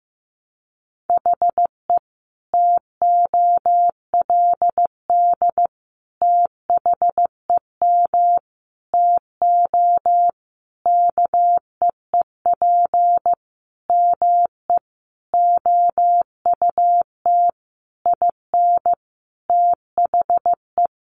Morse code, heard as a keyed tone, about 15 wpm, 700 Hz. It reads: HE TOLD THEM TO KEEP ME OUT IN THE
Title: Sentences from Top 1200 Words 15wpm